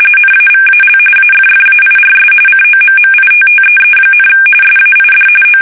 RTTY, about 45.45 baud. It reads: NAVAL AFFAIRS COMMITTEE, SAID O
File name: fsk850 / battleships